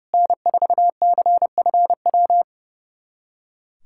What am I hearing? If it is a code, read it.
Morse code, heard as a keyed tone, about 32 wpm, 700 Hz. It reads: N4CFW
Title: USA Callsign DE Callsign 30wpm